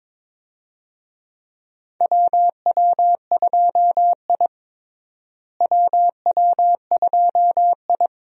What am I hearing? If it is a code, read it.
Morse code, heard as a keyed tone, about 22 wpm, 700 Hz. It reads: WW2I WW2I
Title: USA Call Signs - 4 Characters - Repeated 2x - 22wpm